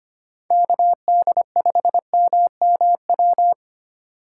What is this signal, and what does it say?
Morse code, heard as a keyed tone, about 28 wpm, 700 Hz. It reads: KD5MMW
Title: Speed Racing - USA 2x3 Call Signs 25wpm